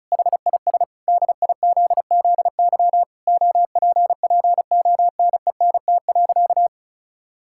Morse code, heard as a keyed tone, about 36 wpm, 700 Hz. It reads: HIS DIZZY OPPONENT.
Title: Pangram Challenge v2 - 35wpm